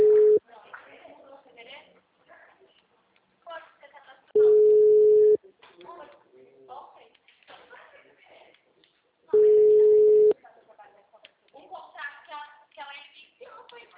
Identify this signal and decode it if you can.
Ringback tone (Europe, 425 Hz), with speech.